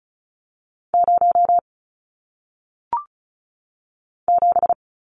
Morse code, heard as a keyed tone, about 35 wpm, 700 Hz. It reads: 0 7
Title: Numbers - 1-Digit 35wpm